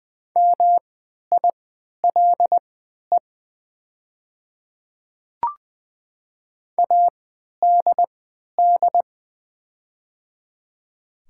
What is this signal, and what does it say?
Morse code, heard as a keyed tone, about 20 wpm, spaced as for 11 wpm, 690 Hz. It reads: MILE ADD
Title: 3x Character Spacing - Top 300 Words - 20wpm